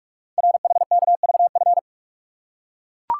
Morse code, heard as a keyed tone, about 44 wpm, 690 Hz. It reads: RHKVF
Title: Code Groups - 5 Letters - 45wpm